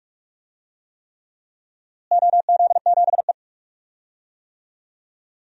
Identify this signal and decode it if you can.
Morse code, heard as a keyed tone, about 47 wpm, 700 Hz. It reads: OZ7E
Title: DXCC-100 Call Signs - 4 Characters - Repeated 3x - 45wpm